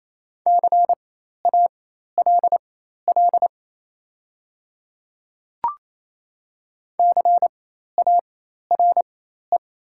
Morse code, heard as a keyed tone, about 28 wpm, spaced as for 13 wpm, 700 Hz. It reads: CALL CARE
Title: 4x Character Spacing - Top 500 Words - 28wpm